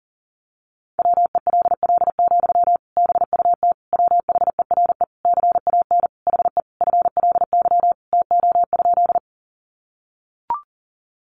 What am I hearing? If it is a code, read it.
Morse code, heard as a keyed tone, about 40 wpm, 700 Hz. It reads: WELL, BUT WHERE CAN HE FLY TO?
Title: Sentences from Top 500 Words 40wpm